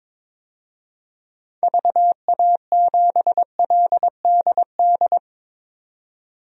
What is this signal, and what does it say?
Morse code, heard as a keyed tone, about 22 wpm, 700 Hz. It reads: VA7LDD
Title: DXCC-100 Call Signs - 6 Characters - 22wpm